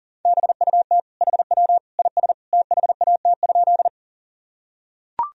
Morse code, heard as a keyed tone, about 40 wpm, 700 Hz. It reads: BUT HW IS THAT?
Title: Sentences from Top 100 Words and Abbreviations 40wpm